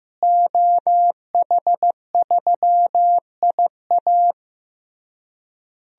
Morse code, heard as a keyed tone, about 16 wpm, 700 Hz. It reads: OH3IA
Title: DXCC-100 Call Signs - 5 Characters - Repeated 3x - 15wpm